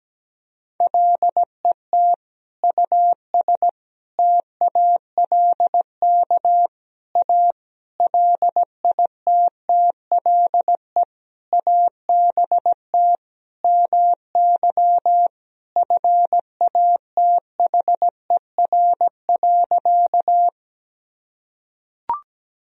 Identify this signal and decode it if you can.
Morse code, heard as a keyed tone, about 17 wpm, 700 Hz. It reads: LET US TALK A LITTLE ABT MY FATHER.
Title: Sentences from Top 400 Words and Abbreviations 17wpm